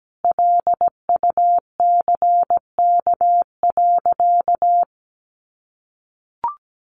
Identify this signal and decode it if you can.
Morse code, heard as a keyed tone, about 17 wpm, 700 Hz. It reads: LUCK.
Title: Sentences from Top 1500 Words 17wpm